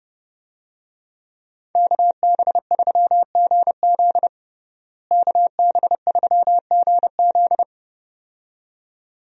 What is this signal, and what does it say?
Morse code, heard as a keyed tone, about 30 wpm, 700 Hz. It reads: KB3GZ KB3GZ
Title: USA Call Signs - 5 Characters - Repeated 2x - 30wpm